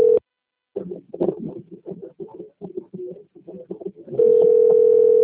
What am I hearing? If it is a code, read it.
Ringback tone (North America), 440 + 480 Hz.